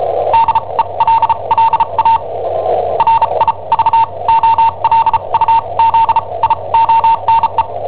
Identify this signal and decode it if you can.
Morse code, heard as a keyed tone, about 34 wpm, spaced as for 27 wpm, 910 Hz. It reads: DELLA RIVOLUZIONE
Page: Cassetta esercizi preparata da I7OHP per IW5ELP